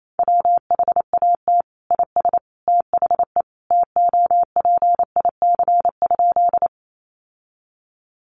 Morse code, heard as a keyed tone, about 28 wpm, 700 Hz. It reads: WHAT IS THE TOPIC?
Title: Sentences from Top 1200 Words 28wpm